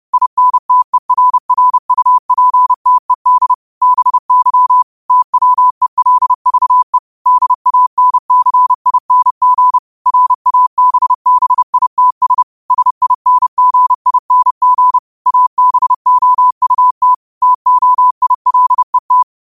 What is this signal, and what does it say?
Morse code, heard as a keyed tone, about 30 wpm, 1000 Hz. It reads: INTERRUPTED BY TWELVE DANCING RABBITS SINGING ABOUT TOILET
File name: 30wpm